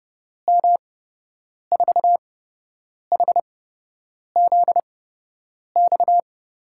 Morse code, heard as a keyed tone, about 30 wpm, spaced as for 8 wpm, 700 Hz. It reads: M4HZX
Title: USA Call Signs - 6 Characters - 8x Character Spacing - 30wpm